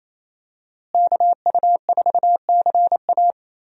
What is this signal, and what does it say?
Morse code, heard as a keyed tone, about 27 wpm, 700 Hz. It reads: KU4CA
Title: USA Call Signs - 5 Characters - Repeated 3x - 28wpm